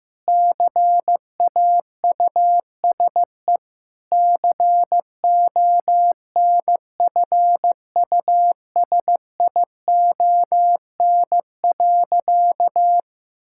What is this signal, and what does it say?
Morse code, heard as a keyed tone, about 16 wpm, 700 Hz. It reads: CAUSE CONFUSION.
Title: Sentences from Top 1600 Words 15wpm